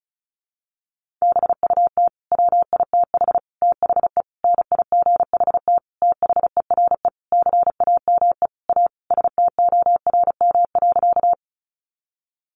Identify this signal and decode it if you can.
Morse code, heard as a keyed tone, about 34 wpm, 700 Hz. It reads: BUT WITH THE NIGHT THERE CAME A STORM.
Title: Sentences from Top 1500 Words 35wpm